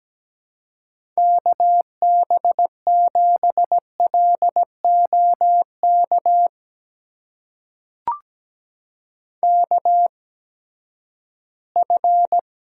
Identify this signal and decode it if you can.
Morse code, heard as a keyed tone, about 17 wpm, spaced as for 12 wpm, 700 Hz. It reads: KB7LOK KF
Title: USA Call Signs - 6 Characters - 8x Character Spacing - 17wpm